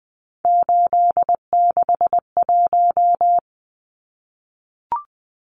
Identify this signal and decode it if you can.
Morse code, heard as a keyed tone, about 20 wpm, 700 Hz. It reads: 861
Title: Numbers - 3-Digit 20wpm